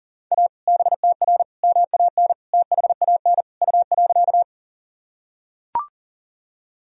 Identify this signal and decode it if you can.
Morse code, heard as a keyed tone, about 38 wpm, 700 Hz. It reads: A BTR MAN THAN U.
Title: Sentences from Top 400 Words and Abbreviations 40wpm